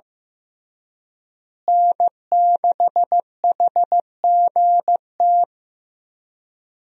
Morse code, heard as a keyed tone, about 16 wpm, 700 Hz. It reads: N6HGT
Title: USA Call Signs - 5 Characters - Repeated 3x - 15wpm